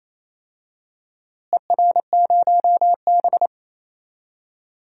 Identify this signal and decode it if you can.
Morse code, heard as a keyed tone, about 28 wpm, 700 Hz. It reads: ER0B
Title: DXCC-100 Call Signs - 4 Characters - Repeated 3x - 28wpm